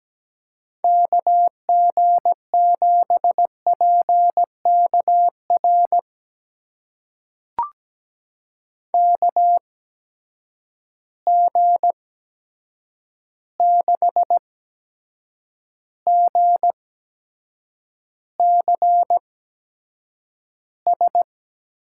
Morse code, heard as a keyed tone, about 17 wpm, spaced as for 7 wpm, 700 Hz. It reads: KG7PKR KG6GCS
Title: USA Call Signs - 6 Characters - 8x Character Spacing - 17wpm